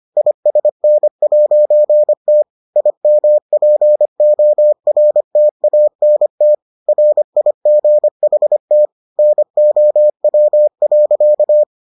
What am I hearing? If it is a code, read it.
Morse code, heard as a keyed tone, about 25 wpm, 600 Hz. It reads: ISN'T IMPORTANT RIGHT NOW.